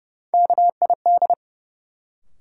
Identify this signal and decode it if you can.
Morse code, heard as a keyed tone, about 30 wpm, 700 Hz. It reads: KID
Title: Sentences from Top 1600 Words - New Word Drill Alt - 30wpm